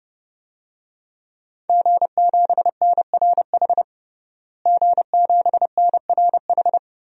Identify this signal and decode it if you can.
Morse code, heard as a keyed tone, about 32 wpm, 700 Hz. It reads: G7NRH G7NRH
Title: DXCC-100 Call Signs - 5 Characters - Repeated 3x - 30wpm